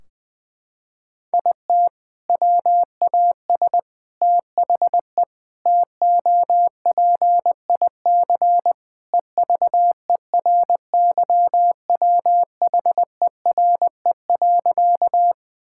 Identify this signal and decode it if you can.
Morse code, heard as a keyed tone, about 20 wpm, 700 Hz. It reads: IT WAS THE TOPIC EVERYWHERE.
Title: Sentences from Top 1600 Words 20wpm